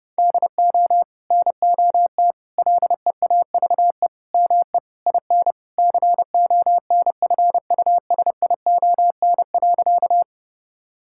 Morse code, heard as a keyed tone, about 30 wpm, 700 Hz. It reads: DO NOT LEAVE ME IN CONFUSION.
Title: Sentences from Top 1600 Words 30wpm